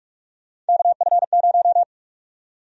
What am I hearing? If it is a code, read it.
Morse code, heard as a keyed tone, about 46 wpm, 700 Hz. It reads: KF0
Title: USA Call Sign Prefixes - Repeated 2x - 45wpm